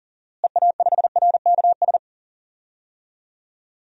Morse code, heard as a keyed tone, about 39 wpm, 700 Hz. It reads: EA5RKS
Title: DXCC-100 Call Signs - 6 Characters - Repeated 2x - 40wpm